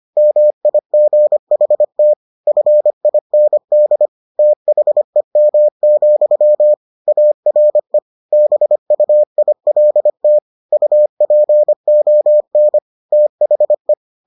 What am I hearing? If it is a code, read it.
Morse code, heard as a keyed tone, about 25 wpm, 600 Hz. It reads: MIGHT FIND THEM, ARE BUILT UPON THE